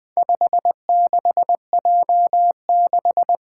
Morse code, heard as a keyed tone, about 20 wpm, 700 Hz. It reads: 56J6
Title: Four Letter-Number Combinations - 20wpm